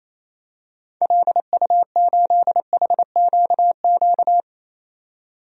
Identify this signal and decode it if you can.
Morse code, heard as a keyed tone, about 28 wpm, 700 Hz. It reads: LU8HQQ
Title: DXCC-100 Call Signs - 6 Characters - 28wpm